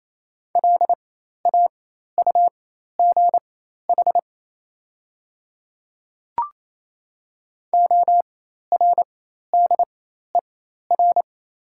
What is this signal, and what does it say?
Morse code, heard as a keyed tone, about 28 wpm, spaced as for 13 wpm, 690 Hz. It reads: LAUGH ORDER
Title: 4x Character Spacing - Top 500 Words - 28wpm